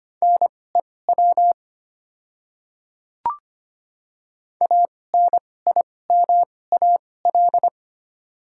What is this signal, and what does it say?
Morse code, heard as a keyed tone, about 25 wpm, spaced as for 18 wpm, 700 Hz. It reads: NEW ANIMAL
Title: 2x Character Spacing - Top 200 Words - 25wpm